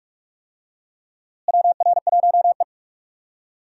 Morse code, heard as a keyed tone, about 47 wpm, 700 Hz. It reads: WR1E
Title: USA Call Signs - 4 Characters - Repeated 3x - 45wpm